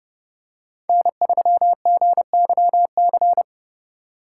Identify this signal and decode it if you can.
Morse code, heard as a keyed tone, about 29 wpm, 700 Hz. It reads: N3GYC